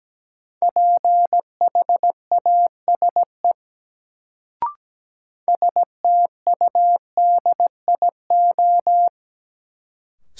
Morse code, heard as a keyed tone, about 17 wpm, 700 Hz. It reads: PHASE STUDIO
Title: Sentences from Top 1200 Words - New Word Drill - 17wpm